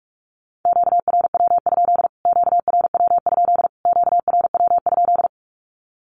Morse code, heard as a keyed tone, about 44 wpm, 700 Hz. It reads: QRW? QRW? QRW?
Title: Q-Codes Repeated 3x 45wpm